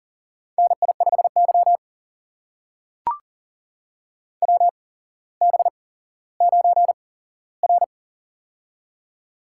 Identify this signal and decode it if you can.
Morse code, heard as a keyed tone, about 40 wpm, spaced as for 17 wpm, 700 Hz. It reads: NI5Y WB9R
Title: USA Call Signs - 8x Character Spacing - 40wpm